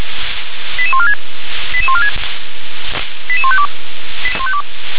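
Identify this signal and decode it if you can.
Five-tone selective call (ZVEI), heard as quick runs of five tones, tones of about 70 ms. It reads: 88146 88146 88152
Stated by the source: ZVEI five-tone selective call 88912 received on SDR.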